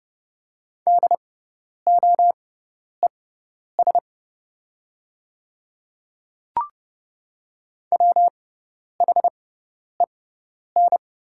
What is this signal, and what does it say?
Morse code, heard as a keyed tone, about 30 wpm, spaced as for 10 wpm, 690 Hz. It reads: DOES WHEN
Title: 6x Character Spacing - Top 200 Words - 30wpm